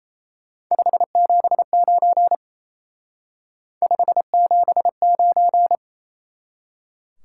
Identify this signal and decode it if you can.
Morse code, heard as a keyed tone, about 36 wpm, 700 Hz. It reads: 579 579